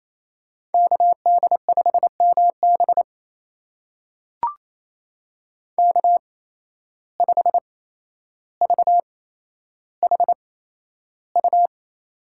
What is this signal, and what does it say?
Morse code, heard as a keyed tone, about 28 wpm, spaced as for 12 wpm, 700 Hz. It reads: KD5MB K5VHU
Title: USA Call Signs - 5 Characters - 8x Character Spacing - 28wpm